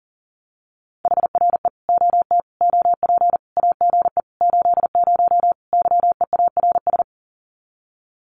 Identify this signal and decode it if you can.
Morse code, heard as a keyed tone, about 40 wpm, 700 Hz. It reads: HRE OT OP AGE 80 YEARS
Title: Common QSO Phrases 40wpm